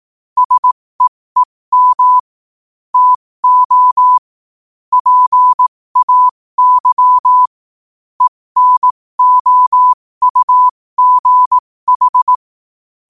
Morse code, heard as a keyed tone, about 18 wpm, spaced as for 15 wpm, 1000 Hz. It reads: SEEM TO PAY ENOUGH
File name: qotdfiles / 15wpm